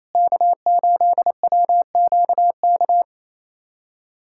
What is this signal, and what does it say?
Morse code, heard as a keyed tone, about 28 wpm, 700 Hz. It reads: K8WQK
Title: USA Callsign DE Callsign 28wpm